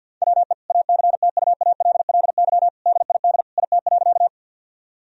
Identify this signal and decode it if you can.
Morse code, heard as a keyed tone, about 50 wpm, 700 Hz. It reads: WE ACTUALLY DID IT.
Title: Sentences from Top 1000 Words - Review Part 4 - 50wpm